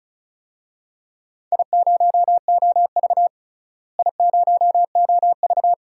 Morse code, heard as a keyed tone, about 36 wpm, 700 Hz. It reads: I0OV I0OV
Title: DXCC-100 Call Signs - 4 Characters - Repeated 2x - 35wpm